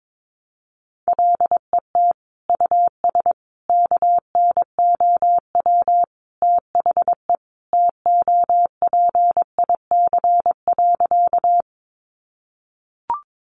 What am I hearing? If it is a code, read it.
Morse code, heard as a keyed tone, about 22 wpm, 700 Hz. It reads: LET US KNOW THE TOPIC.
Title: Sentences from Top 1200 Words 22wpm